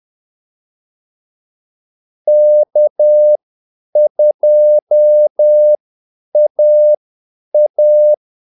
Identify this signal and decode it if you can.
Morse code, heard as a keyed tone, about 10 wpm, spaced as for 8 wpm, 600 Hz. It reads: K2AA
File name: Callsigns-10-8wpm